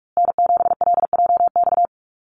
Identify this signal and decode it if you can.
Morse code, heard as a keyed tone, about 44 wpm, 700 Hz. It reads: N7LJX
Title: USA Call Signs Repeated 2x 45wpm